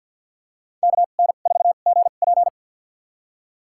Morse code, heard as a keyed tone, about 53 wpm, spaced as for 37 wpm, 700 Hz. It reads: KN4GP
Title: USA Call Signs - 5 Characters - 2x Character Spacing - 50wpm